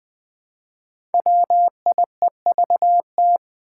Morse code, heard as a keyed tone, about 20 wpm, 700 Hz. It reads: WIEVT
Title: Five Challenging Letter-Number Combinations 20wpm